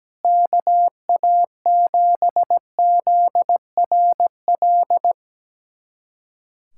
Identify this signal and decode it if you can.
Morse code, heard as a keyed tone, about 17 wpm, 700 Hz. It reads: KA7ZRL
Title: USA Callsign DE Callsign 17wpm